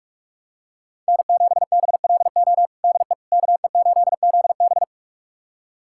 Morse code, heard as a keyed tone, about 46 wpm, 700 Hz. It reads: N7BLO DE KE8ZB
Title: USA Callsign DE Callsign - 5 Characters - 45wpm